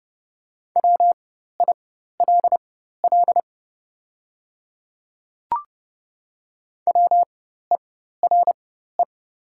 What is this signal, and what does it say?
Morse code, heard as a keyed tone, about 30 wpm, spaced as for 14 wpm, 700 Hz. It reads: WILL WERE
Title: 4x Character Spacing - Top 100 Words - 30wpm